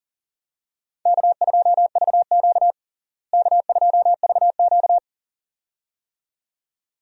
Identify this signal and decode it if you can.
Morse code, heard as a keyed tone, about 40 wpm, 700 Hz. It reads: K2VQ K2VQ
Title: USA Call Signs - 4 Characters - Repeated 2x - 40wpm